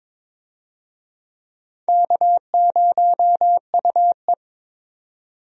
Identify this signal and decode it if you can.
Morse code, heard as a keyed tone, about 22 wpm, 700 Hz. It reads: K0UE
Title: USA Call Signs - 4 Characters - 22wpm